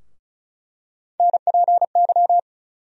Morse code, heard as a keyed tone, about 36 wpm, 700 Hz. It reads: NPY
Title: USA Call Sign Postfixes - 3 Letters 35wpm